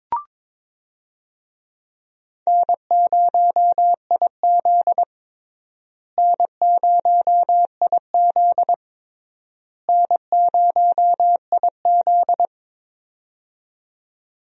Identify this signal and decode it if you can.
Morse code, heard as a keyed tone, about 22 wpm, 700 Hz. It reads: N0IZ N0IZ N0IZ
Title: USA Call Signs - 4 Characters - Repeated 3x - 22wpm